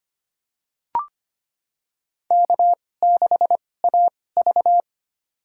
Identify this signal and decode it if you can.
Morse code, heard as a keyed tone, about 25 wpm, spaced as for 18 wpm, 700 Hz. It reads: K6AV
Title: USA Call Signs - 2x Character Spacing - 25wpm